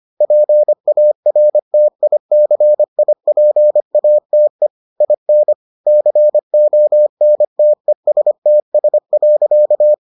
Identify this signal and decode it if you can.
Morse code, heard as a keyed tone, about 25 wpm, 600 Hz. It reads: PARTICIPATE IN CONTESTS.